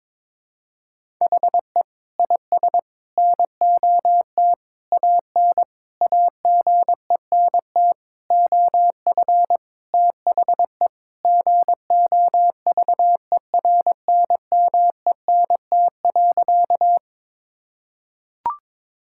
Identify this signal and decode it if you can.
Morse code, heard as a keyed tone, about 22 wpm, 700 Hz. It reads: HE IS NOT AN AGENT OF THE GOVERNMENT.
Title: Sentences from Top 1600 Words 22wpm